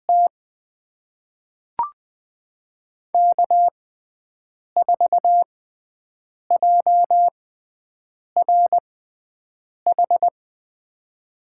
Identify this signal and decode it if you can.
Morse code, heard as a keyed tone, about 21 wpm, spaced as for 7 wpm, 700 Hz. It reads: T K4JRH
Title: USA Call Signs - 5 Characters - 6x Character Spacing - 20wpm